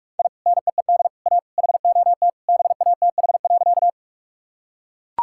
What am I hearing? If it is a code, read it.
Morse code, heard as a keyed tone, about 41 wpm, 700 Hz. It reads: I NEED A HOT BATH.
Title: Sentences from Top 1600 Words 45wpm